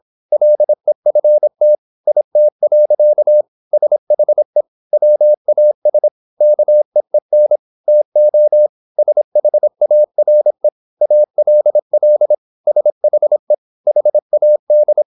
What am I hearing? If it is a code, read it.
Morse code, heard as a keyed tone, about 26 wpm, 600 Hz. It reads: LEFT IT. SHE WAS KEEN TO SHARE ALL SHE HAD
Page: Never 26wpm | CW med Gnister